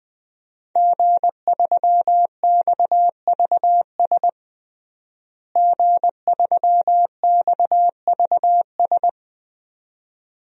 Morse code, heard as a keyed tone, about 21 wpm, 700 Hz. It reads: G3XVS G3XVS
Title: DXCC-100 Call Signs - 5 Characters - Repeated 2x - 20wpm